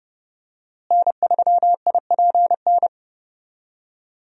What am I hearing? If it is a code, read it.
Morse code, heard as a keyed tone, about 30 wpm, 700 Hz. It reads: N3IPN
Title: USA Call Signs - 5 Characters - Repeated 3x - 30wpm